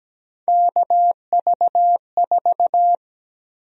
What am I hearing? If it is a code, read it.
Morse code, heard as a keyed tone, about 17 wpm, 700 Hz. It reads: KV4
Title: USA Call Sign Prefixes - 17wpm